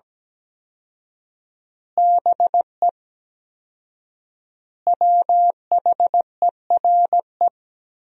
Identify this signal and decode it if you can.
Morse code, heard as a keyed tone, about 17 wpm, 700 Hz. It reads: BE WHERE
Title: 4x Word-Spacing - Sentences from Top 200 Words - 17wpm